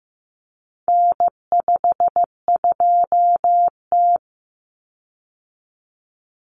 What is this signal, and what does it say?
Morse code, heard as a keyed tone, about 15 wpm, 700 Hz. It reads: N52T
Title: Common Cut Numbers - 4-Digit 15wpm